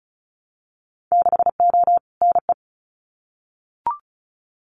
Morse code, heard as a keyed tone, about 34 wpm, 700 Hz. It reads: 6O NE
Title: Winter Field Day Exchanges Encore 35wpm